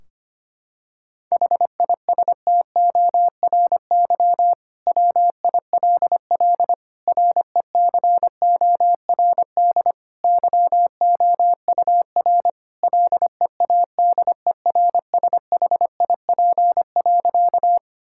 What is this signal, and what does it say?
Morse code, heard as a keyed tone, about 25 wpm, 700 Hz. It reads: HISTORY WILL RECORD YOUR LEADERSHIP.
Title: Sentences from Top 1600 Words 25wpm